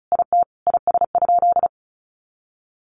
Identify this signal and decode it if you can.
Morse code, heard as a keyed tone, about 34 wpm, 700 Hz. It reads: IT IS?
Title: Sentences from Top 100 Words 35wpm